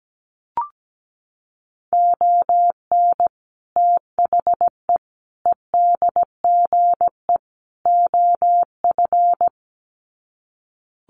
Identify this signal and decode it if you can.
Morse code, heard as a keyed tone, about 17 wpm, 700 Hz. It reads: ON THE EDGE OF
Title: Sets of 4 Words 17wpm